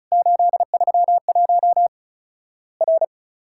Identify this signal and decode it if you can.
Morse code, heard as a keyed tone, about 34 wpm, 700 Hz. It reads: 831 R
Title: Worked All Europe DX Contest - QTC Exchange - 35wpm